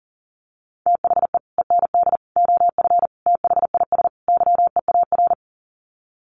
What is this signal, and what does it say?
Morse code, heard as a keyed tone, about 40 wpm, 700 Hz. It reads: THE END OF THIS YEAR
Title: Sets of 5 Words Encore 40wpm